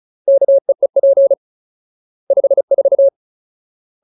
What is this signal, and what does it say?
Morse code, heard as a keyed tone, about 34 wpm, 550 Hz. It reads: KEEP 54
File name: Ultimate Word List / 35WPM Ultimate